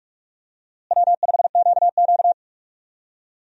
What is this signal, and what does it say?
Morse code, heard as a keyed tone, about 42 wpm, 700 Hz. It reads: W5QQ
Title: USA Call Signs - 4 Characters - Repeated 3x - 45wpm